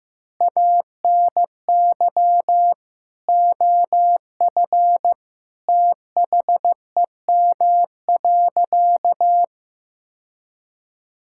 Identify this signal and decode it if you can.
Morse code, heard as a keyed tone, about 15 wpm, 700 Hz. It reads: ANY OF THEM.
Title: Sentences from Top 1000 Words - Review Part 6 - 15wpm